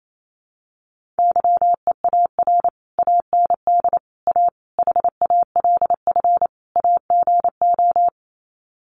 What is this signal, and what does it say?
Morse code, heard as a keyed tone, about 28 wpm, 700 Hz. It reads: YEAR AND A HALF AGO
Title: Sets of 5 Words Encore 28wpm